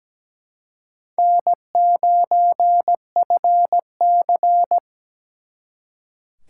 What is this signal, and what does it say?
Morse code, heard as a keyed tone, about 20 wpm, 700 Hz. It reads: N9FC
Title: Speed Racing - USA 1x2 Call Signs 17wpm